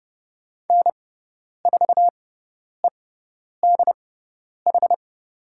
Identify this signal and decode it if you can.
Morse code, heard as a keyed tone, about 30 wpm, spaced as for 10 wpm, 700 Hz. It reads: N4EDH
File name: Lesson 18g - Letter m - Call Signs 30wpm